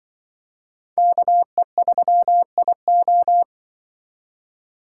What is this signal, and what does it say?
Morse code, heard as a keyed tone, about 24 wpm, 700 Hz. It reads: KE3IO
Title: Speed Racing - USA Call Signs 20wpm